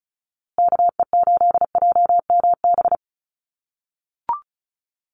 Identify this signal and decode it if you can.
Morse code, heard as a keyed tone, about 34 wpm, 700 Hz. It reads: KE8JMB
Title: USA Call Signs Repeated 3x 35wpm